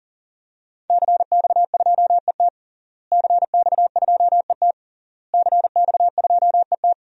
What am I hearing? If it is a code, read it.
Morse code, heard as a keyed tone, about 40 wpm, 700 Hz. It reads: CX2ET CX2ET CX2ET
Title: DXCC-100 Call Signs - 5 Characters - Repeated 3x - 40wpm